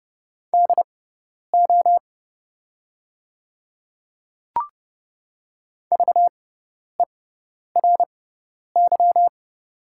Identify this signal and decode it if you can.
Morse code, heard as a keyed tone, about 32 wpm, spaced as for 10 wpm, 690 Hz. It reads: DO VERY
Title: 6x Character Spacing - Top 500 Words - 30wpm